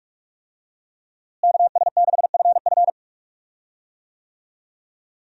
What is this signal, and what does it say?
Morse code, heard as a keyed tone, about 46 wpm, 700 Hz. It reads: KS6FF
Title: USA Call Signs - 5 Characters - Repeated 2x - 45wpm